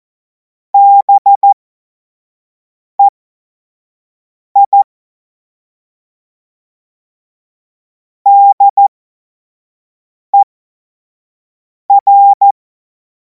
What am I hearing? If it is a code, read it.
Morse code, heard as a keyed tone, about 14 wpm, spaced as for 5 wpm, 800 Hz. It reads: BEI DER
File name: Morsekurs / Witze / Witz 14. Juli 2026 25 F70